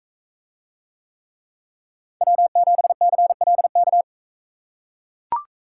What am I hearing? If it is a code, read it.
Morse code, heard as a keyed tone, about 42 wpm, 700 Hz. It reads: W7CLK